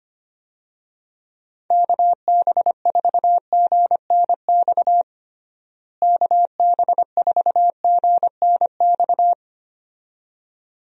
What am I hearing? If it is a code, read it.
Morse code, heard as a keyed tone, about 25 wpm, 700 Hz. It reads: KB4GNX KB4GNX
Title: USA Call Signs - 6 Characters - Repeated 2x - 25wpm